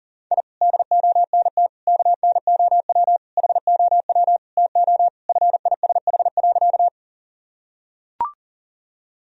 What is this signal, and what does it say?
Morse code, heard as a keyed tone, about 40 wpm, 700 Hz. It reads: I DONT KNOW HOW TO FISH.
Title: Sentences from Top 1000 Words - Review Part 1 - 40wpm